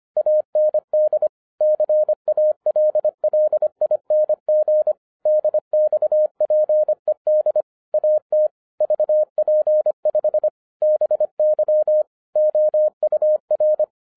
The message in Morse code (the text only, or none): AND CALLING DXPED AT VP5 BY OUR